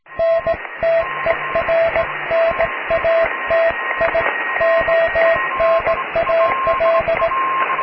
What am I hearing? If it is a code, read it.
Morse code, heard as a keyed tone, about 18 wpm, 660 Hz, with radio noise. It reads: NTERNATIONAL
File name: cw / tekst(b)+QRM